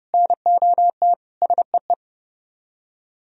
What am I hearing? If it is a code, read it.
Morse code, heard as a keyed tone, about 37 wpm, 700 Hz. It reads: NOT SEE
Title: Speed-Racing - Sets of 3 Words 30wpm